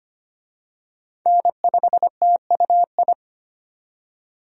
Morse code, heard as a keyed tone, about 25 wpm, 700 Hz. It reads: N5TUI
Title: USA Call Signs - 5 Characters - 25wpm